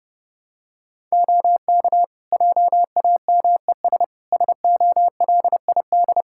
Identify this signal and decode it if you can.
Morse code, heard as a keyed tone, about 30 wpm, 700 Hz. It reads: OK JAMES SOLID
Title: Practice QSO - Exchange 3 - More Info - 30wpm